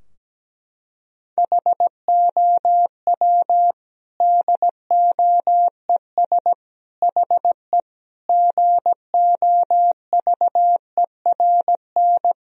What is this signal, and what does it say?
Morse code, heard as a keyed tone, about 17 wpm, 700 Hz. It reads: HOW DOES HE GOVERN
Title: Sentences from Top 1000 Words - Review Part 2 - 17wpm